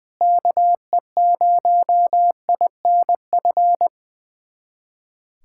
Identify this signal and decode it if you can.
Morse code, heard as a keyed tone, about 20 wpm, 700 Hz. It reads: KE0INF
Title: USA Callsign DE Callsign 20wpm